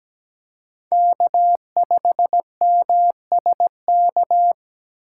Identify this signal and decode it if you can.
Morse code, heard as a keyed tone, about 20 wpm, 700 Hz. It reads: K5MSK
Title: Speed Racing - USA Call Signs 17wpm